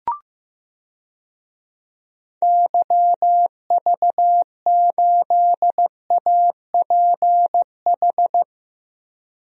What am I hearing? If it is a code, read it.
Morse code, heard as a keyed tone, about 15 wpm, 700 Hz. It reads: YV8APH
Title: DXCC-100 Call Signs - 6 Characters - Repeated 3x - 15wpm